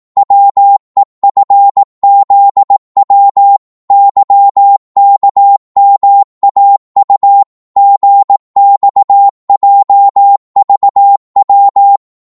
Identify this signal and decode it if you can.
Morse code, heard as a keyed tone, about 18 wpm, 800 Hz. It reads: WEFZW YKMAU GXJVW
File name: Morse / ltr-18wpm